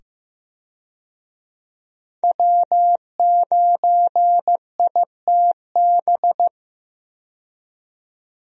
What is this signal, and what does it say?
Morse code, heard as a keyed tone, about 15 wpm, 700 Hz. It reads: W9ITB
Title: USA Call Signs - 5 Characters - 15wpm